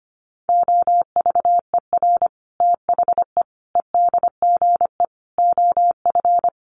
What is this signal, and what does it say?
Morse code, heard as a keyed tone, about 25 wpm, 700 Hz. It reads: OVER THE EDGE OF
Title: Sets of 5 Words Encore2 25wpm